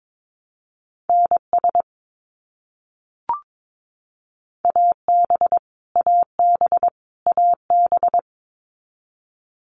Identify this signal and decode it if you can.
Morse code, heard as a keyed tone, about 22 wpm, 700 Hz. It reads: NS AB AB AB
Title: Canadian Provinces and Territories - Repeated 3x - 22wpm